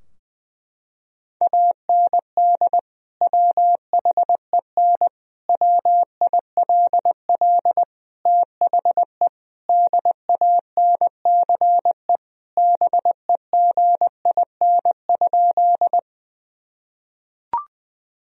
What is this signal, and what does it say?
Morse code, heard as a keyed tone, about 20 wpm, 700 Hz. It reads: AND WHEN WILL THE DANCE BEGIN?
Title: Sentences from Top 1000 Words - Review Part 4 - 20wpm